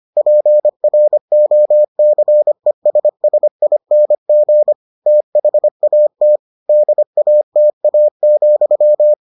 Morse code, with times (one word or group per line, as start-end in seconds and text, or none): 0.2-4.7 PROCESSING
5.1-6.4 THAT
6.7-9.1 DATA,